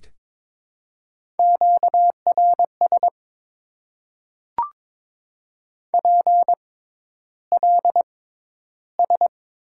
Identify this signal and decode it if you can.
Morse code, heard as a keyed tone, about 22 wpm, spaced as for 11 wpm, 700 Hz. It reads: QRS PLS
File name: Top 90 QSO Elements - 6x Character Spacing - 22wpm